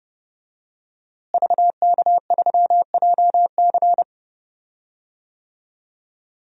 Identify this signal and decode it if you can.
Morse code, heard as a keyed tone, about 30 wpm, 700 Hz. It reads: VK3JC
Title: DXCC-100 Call Signs - 5 Characters - Repeated 2x - 30wpm